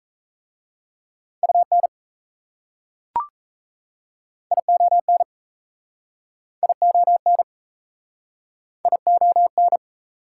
Morse code, heard as a keyed tone, about 35 wpm, 690 Hz. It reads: UN ION ION ION
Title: Speed-Racing Prefixes and Suffixes 28wpm